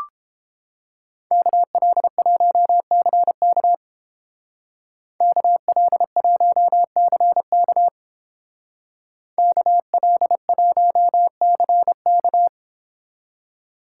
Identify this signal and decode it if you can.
Morse code, heard as a keyed tone, about 28 wpm, 700 Hz. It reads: KL1CK KL1CK KL1CK